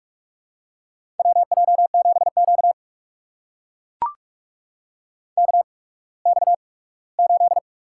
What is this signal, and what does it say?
Morse code, heard as a keyed tone, about 42 wpm, spaced as for 21 wpm, 700 Hz. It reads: WJ7Q KX8
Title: USA Call Signs - 4 Characters - 8x Character Spacing - 45wpm